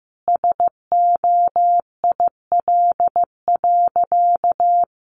Morse code, heard as a keyed tone, about 15 wpm, 700 Hz. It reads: SOIL.
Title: Easy Phrases 15wpm